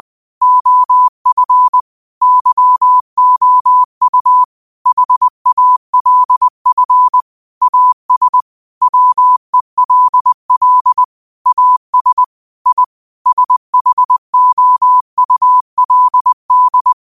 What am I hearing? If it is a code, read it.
Morse code, heard as a keyed tone, about 20 wpm, 1000 Hz. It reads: OF YOU HALF AS WELL AS I SHOULD